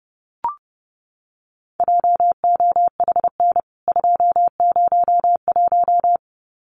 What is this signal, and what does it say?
Morse code, heard as a keyed tone, about 30 wpm, 700 Hz. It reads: JOHN 201
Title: CWT Exchange 30wpm